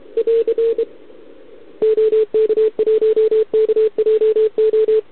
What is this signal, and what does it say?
Morse code, heard as a keyed tone, about 28 wpm, 440 Hz. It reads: <AR> OK1KJO
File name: ARCHIV / OKDX2020 / OK1RIb